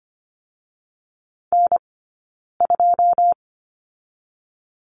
Morse code, heard as a keyed tone, about 25 wpm, 700 Hz. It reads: N 2
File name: course / 25:10wpm / 092 Lesson 16d - Letter c - Two Characters 25wpm